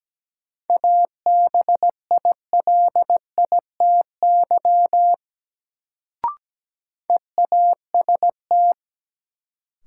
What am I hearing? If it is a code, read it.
Morse code, heard as a keyed tone, about 17 wpm, 700 Hz. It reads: ABILITY EAST
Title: Sentences from Top 1000 Words - Review Part 3 - New Word Drill - 17wpm